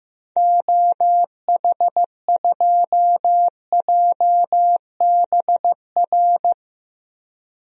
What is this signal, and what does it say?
Morse code, heard as a keyed tone, about 15 wpm, 700 Hz. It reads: OH2JBR
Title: DXCC-100 Call Signs - 6 Characters - 15wpm